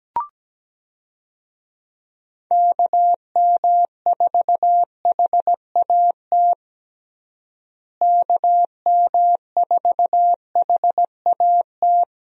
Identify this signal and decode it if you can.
Morse code, heard as a keyed tone, about 17 wpm, 700 Hz. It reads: KM4HAT KM4HAT
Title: USA Call Signs - 6 Characters - Repeated 2x - 17wpm